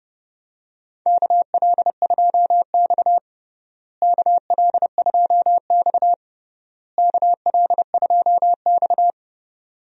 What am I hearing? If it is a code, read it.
Morse code, heard as a keyed tone, about 32 wpm, 700 Hz. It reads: KL2X KL2X KL2X
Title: USA Call Signs - 4 Characters - Repeated 3x - 30wpm